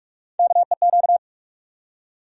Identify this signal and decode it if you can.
Morse code, heard as a keyed tone, about 48 wpm, 700 Hz. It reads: KEQ
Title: USA Call Sign Postfixes - 3 Letters 45wpm